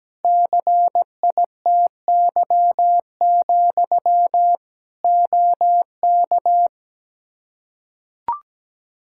Morse code, heard as a keyed tone, about 17 wpm, 700 Hz. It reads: CITY, OK
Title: Practice QSO - Exchange 1 - Beginning - 17wpm